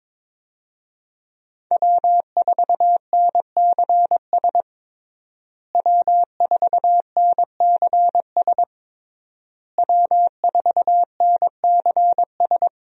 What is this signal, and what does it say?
Morse code, heard as a keyed tone, about 21 wpm, 700 Hz. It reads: W4NCS W4NCS W4NCS
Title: USA Call Signs - 5 Characters - Repeated 3x - 22wpm